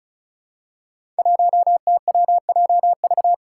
Morse code, heard as a keyed tone, about 34 wpm, 700 Hz. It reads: 1TWJV
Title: Five Challenging Letter-Number Combinations 35wpm